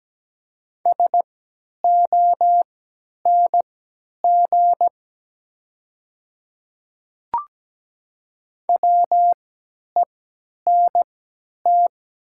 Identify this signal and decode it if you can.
Morse code, heard as a keyed tone, about 17 wpm, spaced as for 10 wpm, 690 Hz. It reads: SONG WENT
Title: 3x Character Spacing - Top 500 Words - 17wpm